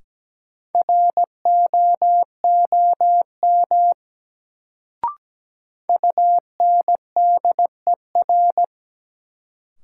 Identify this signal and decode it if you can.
Morse code, heard as a keyed tone, about 17 wpm, 700 Hz. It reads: ROOM UNDER
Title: Sentences from Top 300 Words - New Word Drill - 17wpm